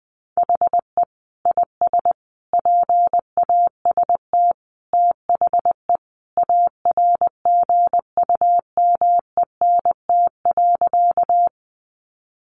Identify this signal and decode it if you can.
Morse code, heard as a keyed tone, about 20 wpm, 700 Hz. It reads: HE IS PAST THE ARGUMENT.
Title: Sentences from Top 1400 Words 20wpm